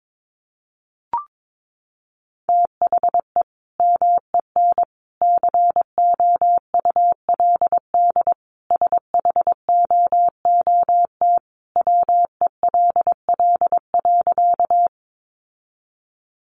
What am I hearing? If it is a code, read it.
Morse code, heard as a keyed tone, about 22 wpm, 700 Hz. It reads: THE MEN COULD SHOOT WELL.
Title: Sentences from Top 1200 Words 22wpm